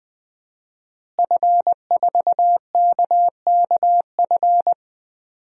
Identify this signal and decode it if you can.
Morse code, heard as a keyed tone, about 21 wpm, 700 Hz. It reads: F4KKF
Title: DXCC-100 Call Signs - 5 Characters - Repeated 2x - 20wpm